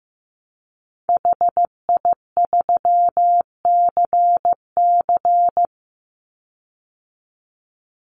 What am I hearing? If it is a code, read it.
Morse code, heard as a keyed tone, about 15 wpm, 700 Hz. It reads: HI3CC
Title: Call Signs Repeated 3x - 15wpm